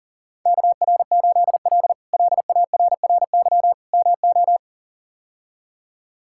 Morse code, heard as a keyed tone, about 40 wpm, 700 Hz. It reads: KR8L LARRY MO
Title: K1USN SST QSO Practice - 40wpm